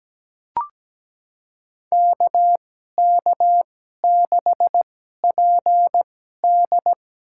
Morse code, heard as a keyed tone, about 17 wpm, spaced as for 12 wpm, 700 Hz. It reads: KK6PD
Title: USA Call Signs - 2x Character Spacing - 17wpm